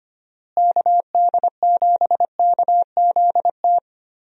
Morse code, heard as a keyed tone, about 25 wpm, 700 Hz. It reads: KD7KZT
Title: USA Call Signs - 6 Characters - Repeated 2x - 25wpm